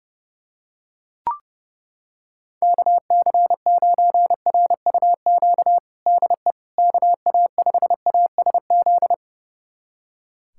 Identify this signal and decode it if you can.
Morse code, heard as a keyed tone, about 30 wpm, 700 Hz. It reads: KC9RUQ DE KA5ASZ
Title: USA Callsign DE Callsign - 6 Characters - 30wpm